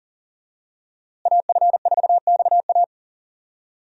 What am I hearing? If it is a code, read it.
Morse code, heard as a keyed tone, about 38 wpm, 700 Hz. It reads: AF4XA
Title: USA Call Signs - 5 Characters - Repeated 2x - 40wpm